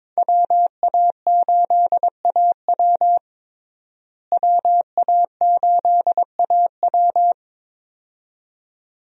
Morse code, heard as a keyed tone, about 21 wpm, 700 Hz. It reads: WA8AW WA8AW
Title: USA Call Signs - 5 Characters - Repeated 2x - 22wpm